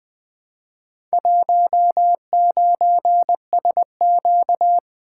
Morse code, heard as a keyed tone, about 20 wpm, 700 Hz. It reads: 19SQ
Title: Four Letter-Number Combinations - 20wpm